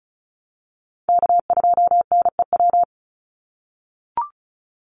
Morse code, heard as a keyed tone, about 34 wpm, 700 Hz. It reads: K2NEW
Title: USA Call Signs 35wpm